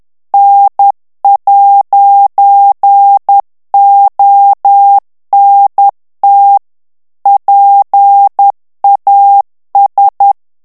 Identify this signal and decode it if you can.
Morse code, heard as a keyed tone, about 10 wpm, 800 Hz. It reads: N'ONT PAS
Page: Français - texte en clair 1 - vitesse : 10 mots minute